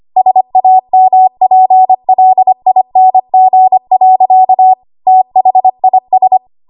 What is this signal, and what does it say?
Morse code, heard as a keyed tone, about 25 wpm, 750 Hz. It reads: SAMPLING. THIS
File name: Archive / 25 WPM / 25WPM